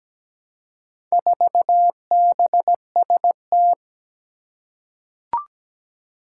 Morse code, heard as a keyed tone, about 17 wpm, 700 Hz. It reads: 4BST
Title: Four Challenging Letter-Number Combinations 17wpm